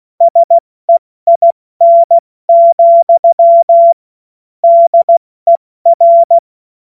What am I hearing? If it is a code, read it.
Morse code, heard as a keyed tone, about 16 wpm, 680 Hz. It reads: SEIN, DER
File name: Morsekurs / Witze / Witz 26 März 2024 70 F80